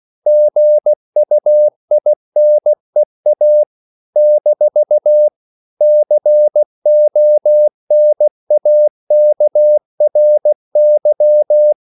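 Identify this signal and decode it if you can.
Morse code, heard as a keyed tone, about 16 wpm, 600 Hz. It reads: GUINEA - CONAKRY
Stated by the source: CW practice of national capitals at 16 words per minute, from ON6ZQ.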